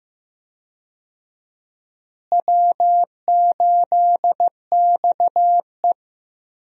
Morse code, heard as a keyed tone, about 15 wpm, 700 Hz. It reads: W8XE
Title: USA Call Signs - 4 Characters - Repeated 2x - 15wpm